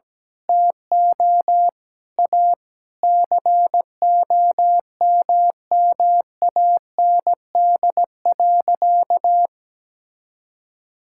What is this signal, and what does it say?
Morse code, heard as a keyed tone, about 17 wpm, 700 Hz. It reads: TO A COMMAND.
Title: Sentences from Top 1600 Words 17wpm